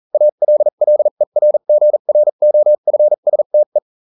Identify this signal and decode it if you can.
Morse code, heard as a keyed tone, about 40 wpm, spaced as for 35 wpm, 600 Hz. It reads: ALLERGROFSTE